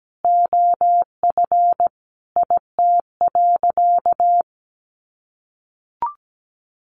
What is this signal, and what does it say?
Morse code, heard as a keyed tone, about 17 wpm, 700 Hz. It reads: OF IT.
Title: Sentences from Top 400 Words 17wpm